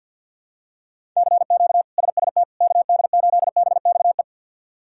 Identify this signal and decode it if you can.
Morse code, heard as a keyed tone, about 52 wpm, 700 Hz. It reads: CQ SST KD8BXE
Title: K1USN SST QSO Practice - 50wpm